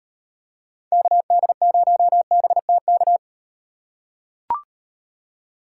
Morse code, heard as a keyed tone, about 38 wpm, 700 Hz. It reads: KD0BTK